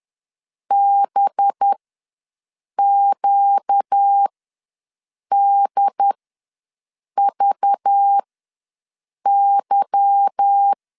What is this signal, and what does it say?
Morse code, heard as a keyed tone, about 10 wpm, spaced as for 6 wpm, 800 Hz. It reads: BQDVY